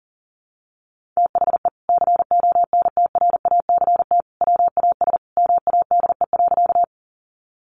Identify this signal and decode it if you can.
Morse code, heard as a keyed tone, about 40 wpm, 700 Hz. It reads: THE CONTRACT WAS MADE.
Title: Sentences from Top 1400 Words 40wpm